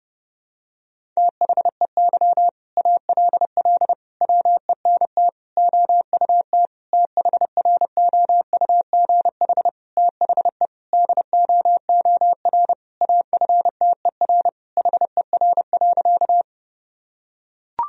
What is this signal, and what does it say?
Morse code, heard as a keyed tone, about 30 wpm, 700 Hz. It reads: THEY ALL WENT OUT THROUGH THE DOOR AFTER HER.
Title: Sentences from Top 1000 Words - Review Part 1 - 30wpm